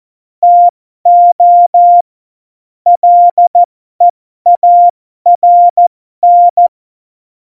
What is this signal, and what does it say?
Morse code, heard as a keyed tone, about 14 wpm, spaced as for 12 wpm, 700 Hz. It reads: TO LEARN
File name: Morsekurs / Witze